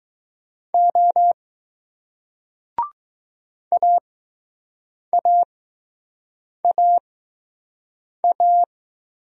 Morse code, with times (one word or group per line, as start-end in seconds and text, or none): 0.7-1.3 O
3.7-8.6 AAAA